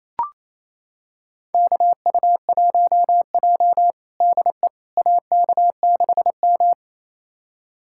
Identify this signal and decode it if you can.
Morse code, heard as a keyed tone, about 28 wpm, 700 Hz. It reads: KU1J DE AK6M
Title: USA Callsign DE Callsign - 4 Characters - 28wpm